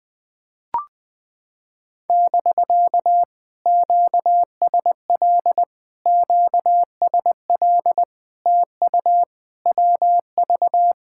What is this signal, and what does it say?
Morse code, heard as a keyed tone, about 20 wpm, 700 Hz. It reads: <BK> QSL QSL TU WV
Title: POTA QSO Practice - 20wpm